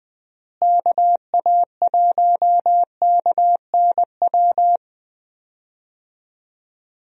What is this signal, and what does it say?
Morse code, heard as a keyed tone, about 20 wpm, 700 Hz. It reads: KA1KNW
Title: USA Call Signs - 6 Characters - Repeated 2x - 20wpm